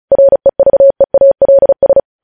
Morse code, heard as a keyed tone, about 36 wpm, 550 Hz. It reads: REVEALS
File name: podcast / News headlines-1x-35x35